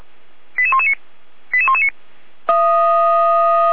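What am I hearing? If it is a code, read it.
Five-tone selective call (ZVEI), heard as quick runs of five tones, tones of about 80 ms. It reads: 88119 88119
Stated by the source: ZVEI five-tone selective call received on an SDR, archived by Feuerwehr Orlamünde fire brigade.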